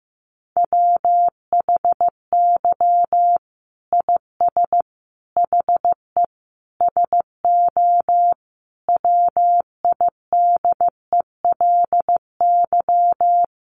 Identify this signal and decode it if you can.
Morse code, heard as a keyed tone, about 15 wpm, 700 Hz. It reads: WHY IS HE SO WIDELY
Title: Sentences from Top 1200 Words 15wpm